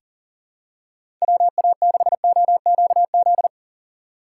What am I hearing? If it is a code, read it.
Morse code, heard as a keyed tone, about 40 wpm, 700 Hz. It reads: WA6OQZ
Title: USA Call Signs - 2x Character Spacing - 40wpm